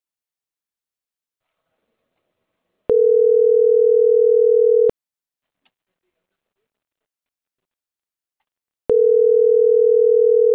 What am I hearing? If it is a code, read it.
Ringback tone (North America), 440 + 480 Hz.